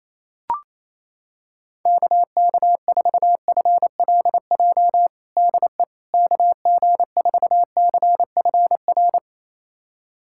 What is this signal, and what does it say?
Morse code, heard as a keyed tone, about 28 wpm, 700 Hz. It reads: KK4FLJ DE KG4CFR
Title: USA Callsign DE Callsign - 6 Characters - 28wpm